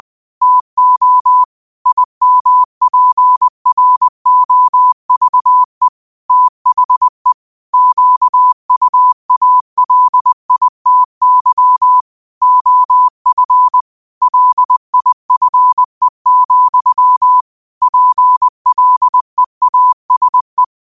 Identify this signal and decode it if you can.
Morse code, heard as a keyed tone, about 20 wpm, 1000 Hz. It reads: TO IMPROVE THE QUALITY OF LIFE, PLEASE
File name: qotdfiles / 20wpm